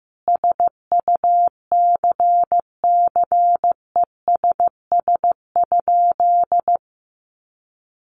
Morse code, heard as a keyed tone, about 15 wpm, 700 Hz. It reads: SUCCESS?
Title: Sentences from Top 900 Words 15wpm